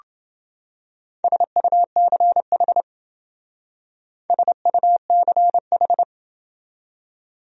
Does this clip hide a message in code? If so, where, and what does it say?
Yes: Morse code from 1.2 to 6.0 s, heard as a keyed tone, about 28 wpm, 700 Hz. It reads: SUCH SUCH